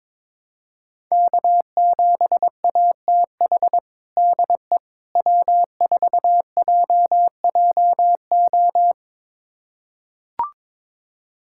Morse code, heard as a keyed tone, about 21 wpm, 700 Hz. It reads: K7ATH DE W4JJO
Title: USA Callsign DE Callsign - 5 Characters - 22wpm